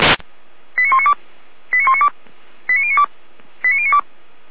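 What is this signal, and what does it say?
Five-tone selective call (ZVEI), heard as quick runs of five tones, tones of about 80 ms. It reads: 89192 89192 89092 89092
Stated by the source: Received on SDR, ZVEI five-tone call 89080.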